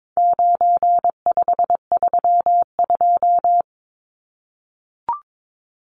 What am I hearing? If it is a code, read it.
Morse code, heard as a keyed tone, about 22 wpm, 700 Hz. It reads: 9532
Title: Numbers - 4-Digit 22wpm